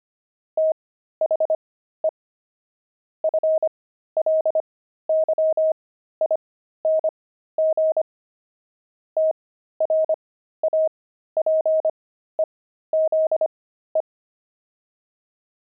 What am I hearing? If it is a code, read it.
Morse code, heard as a keyed tone, about 25 wpm, spaced as for 13 wpm, 630 Hz. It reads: THE FLYING TRAPEZE
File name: pangram-fs-13-cs-25-friars